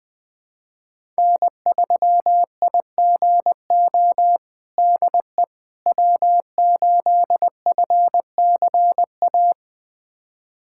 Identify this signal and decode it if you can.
Morse code, heard as a keyed tone, about 20 wpm, 700 Hz. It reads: N3IGO DE W8FCA
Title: USA Callsign DE Callsign - 5 Characters - 20wpm